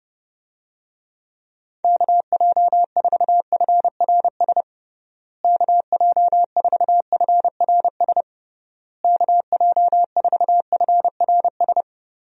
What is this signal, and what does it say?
Morse code, heard as a keyed tone, about 30 wpm, 700 Hz. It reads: KJ4FRS KJ4FRS KJ4FRS
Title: USA Call Signs - 6 Characters - Repeated 3x - 30wpm